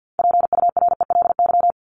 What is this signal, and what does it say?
Morse code, heard as a keyed tone, about 50 wpm, 700 Hz. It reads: PURELY
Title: LY Suffix - Top 100 Words with Base Words 50wpm